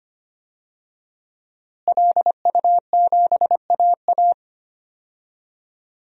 Morse code, heard as a keyed tone, about 25 wpm, 700 Hz. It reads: LU7AA
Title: DXCC-100 Call Signs - 5 Characters - Repeated 3x - 25wpm